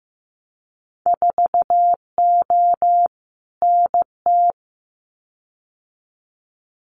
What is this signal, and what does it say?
Morse code, heard as a keyed tone, about 15 wpm, 700 Hz. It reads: 4O NT
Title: Winter Field Day Exchanges Encore 15wpm